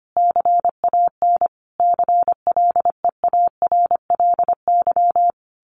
Morse code, heard as a keyed tone, about 25 wpm, 700 Hz. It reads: CAN CLEARLY
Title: Sentences from Top 1400 Words 25wpm